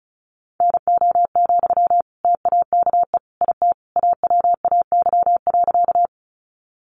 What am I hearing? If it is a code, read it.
Morse code, heard as a keyed tone, about 34 wpm, 700 Hz. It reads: NO, TAKE IT AWAY.
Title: Sentences from Top 600 Words 35wpm